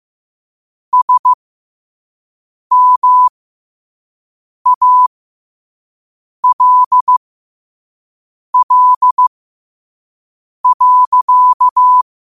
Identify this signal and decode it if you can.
Morse code, heard as a keyed tone, about 15 wpm, spaced as for 5 wpm, 1000 Hz. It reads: SMALL.
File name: qotdfiles / 5wpm